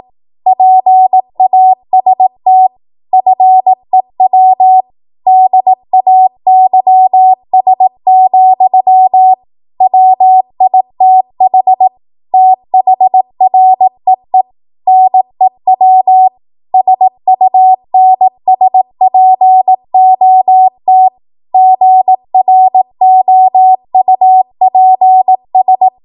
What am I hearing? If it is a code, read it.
Morse code, heard as a keyed tone, about 18 wpm, 750 Hz. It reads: PAST FEW DAYS, WITH THREE NEW SUNSPOT GROUPS
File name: Archive / 18 WPM / 18WPM